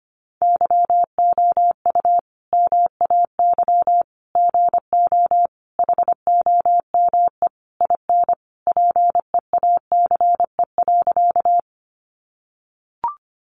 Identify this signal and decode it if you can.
Morse code, heard as a keyed tone, about 25 wpm, 700 Hz. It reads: YOU MAY GO HOME IN PEACE.
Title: Sentences from Top 1500 Words 25wpm